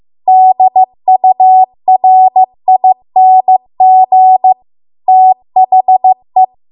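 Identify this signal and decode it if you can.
Morse code, heard as a keyed tone, about 15 wpm, 750 Hz. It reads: DURING THE
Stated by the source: Morse code practice at 15 wpm from the national amateur radio association's code archive.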